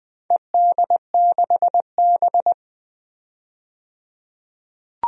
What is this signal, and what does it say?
Morse code, heard as a keyed tone, about 20 wpm, 700 Hz. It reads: ED6B
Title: DXCC-100 Call Signs - 4 Characters - 20wpm